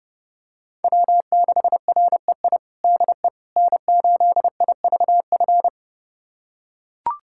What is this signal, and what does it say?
Morse code, heard as a keyed tone, about 32 wpm, 700 Hz. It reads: W6REI DE N8IVF
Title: USA Callsign DE Callsign 30wpm